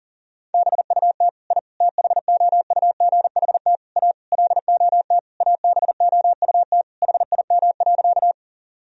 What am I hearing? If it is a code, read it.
Morse code, heard as a keyed tone, about 40 wpm, 700 Hz. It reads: BUT I THOUGHT A LOT ABOUT HIM.
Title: Sentences from Top 1000 Words - Review Part 3 - 40wpm